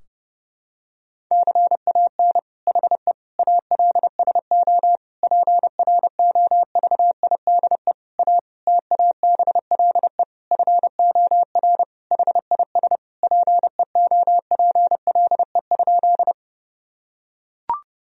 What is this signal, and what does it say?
Morse code, heard as a keyed tone, about 32 wpm, 700 Hz. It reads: CAN HE ALSO PROVIDE A TABLE FOR HIS PEOPLE?
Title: Sentences from Top 1000 Words - Review Part 3 - 30wpm